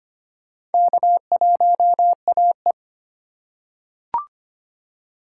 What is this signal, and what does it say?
Morse code, heard as a keyed tone, about 25 wpm, 700 Hz. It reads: K1AE
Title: USA Call Signs - 4 Characters - 2x Character Spacing - 25wpm